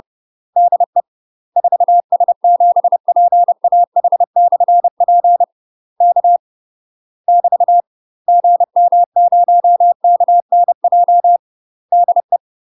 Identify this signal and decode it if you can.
Morse code, heard as a keyed tone, about 30 wpm, 700 Hz. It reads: DE 4S7PAH/P K <BT> GM0KNJ DE
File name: UK MASTER QSOs TRAINING / 30WPM / UK MASTER QSOs 0042